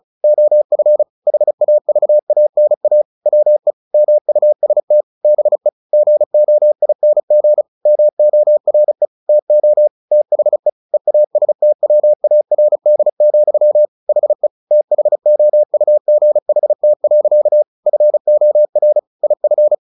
Morse code, heard as a keyed tone, about 34 wpm, 600 Hz. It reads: OF HAVANA WE MUST BE GOING MORE TO THE EASTWARD, HE THOUGHT. FOR IF
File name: the old man and the sea / the old man and the sea 1 35wpm